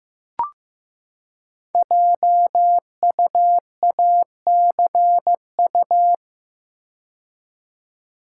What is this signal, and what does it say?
Morse code, heard as a keyed tone, about 15 wpm, 700 Hz. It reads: JUACU
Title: Code Groups - 5 Letters - 15wpm